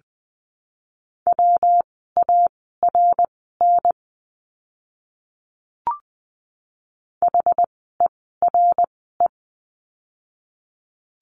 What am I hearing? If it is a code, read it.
Morse code, heard as a keyed tone, about 20 wpm, spaced as for 15 wpm, 700 Hz. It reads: WARN HERE
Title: 115 Lesson 19f - Letter w - Words 20wpm